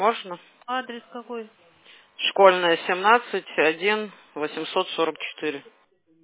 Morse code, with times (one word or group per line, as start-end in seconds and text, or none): none